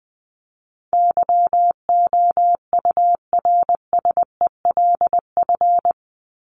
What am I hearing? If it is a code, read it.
Morse code, heard as a keyed tone, about 20 wpm, 700 Hz. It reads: YOURSELF
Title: Top 501-750 Words 20wpm